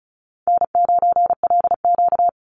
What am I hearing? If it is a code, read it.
Morse code, heard as a keyed tone, about 34 wpm, 700 Hz. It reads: N9LQ
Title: Easy Call Signs Repeated 3x - 35wpm